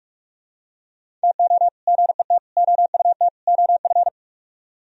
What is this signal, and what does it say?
Morse code, heard as a keyed tone, about 45 wpm, 700 Hz. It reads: TO GET OUT OF